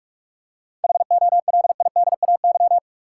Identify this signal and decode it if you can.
Morse code, heard as a keyed tone, about 46 wpm, 700 Hz. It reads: HOLIDAY
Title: Seven Letter Words - 45wpm